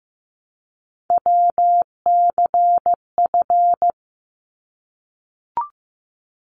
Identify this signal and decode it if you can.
Morse code, heard as a keyed tone, about 15 wpm, 700 Hz. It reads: WCF
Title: ARRL Sections 15wpm